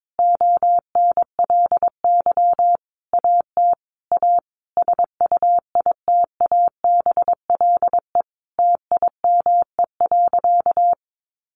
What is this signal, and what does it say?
Morse code, heard as a keyed tone, about 22 wpm, 700 Hz. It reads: ONLY AT A SUITABLE TIME.
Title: Sentences from Top 1400 Words 22wpm